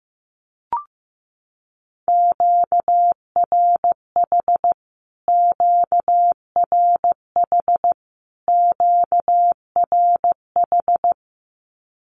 Morse code, heard as a keyed tone, about 15 wpm, 700 Hz. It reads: QRH QRH QRH
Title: Q-Codes Repeated 3x 15wpm